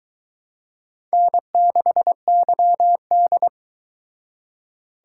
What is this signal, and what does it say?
Morse code, heard as a keyed tone, about 22 wpm, 700 Hz. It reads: N6YD